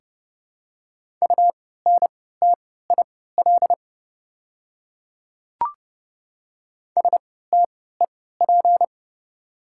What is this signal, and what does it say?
Morse code, heard as a keyed tone, about 30 wpm, spaced as for 17 wpm, 690 Hz. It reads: UNTIL STEP
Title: 3x Character Spacing - Top 500 Words - 30wpm